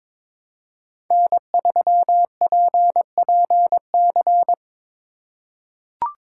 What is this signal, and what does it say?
Morse code, heard as a keyed tone, about 22 wpm, 700 Hz. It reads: N3PPC
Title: USA Call Signs - 3x Character Spacing - 22wpm